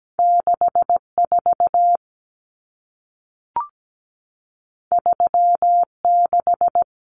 Morse code, heard as a keyed tone, about 17 wpm, 700 Hz. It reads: 64 36
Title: Common Cut Numbers - 2-Digit 17wpm